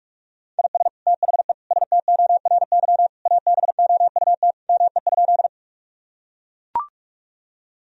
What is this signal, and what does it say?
Morse code, heard as a keyed tone, about 41 wpm, 700 Hz. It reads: IS THE STORY ABOUT ME?
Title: Sentences from Top 1000 Words - Review Part 1 - 45wpm